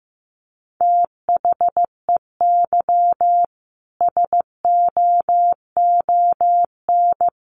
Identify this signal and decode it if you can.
Morse code, heard as a keyed tone, about 15 wpm, 700 Hz. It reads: THEY SOON
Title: Sentences from Top 1200 Words 15wpm